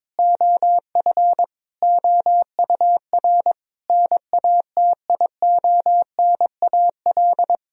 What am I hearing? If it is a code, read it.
Morse code, heard as a keyed tone, about 22 wpm, 700 Hz. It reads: OF OUR NATIONAL
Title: Sentences from Top 1000 Words - Review Part 4 - 22wpm